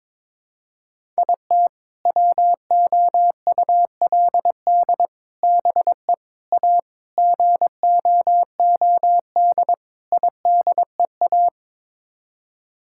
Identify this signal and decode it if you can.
Morse code, heard as a keyed tone, about 22 wpm, 700 Hz. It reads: IT WOULD BE A GOOD IDEA
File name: practice / 22wpm / Sets of 6 Words - 22wpm